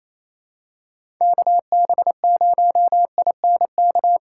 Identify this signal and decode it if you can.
Morse code, heard as a keyed tone, about 28 wpm, 700 Hz. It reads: KB0INK
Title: USA Call Signs - 6 Characters - 8x Character Spacing - 28wpm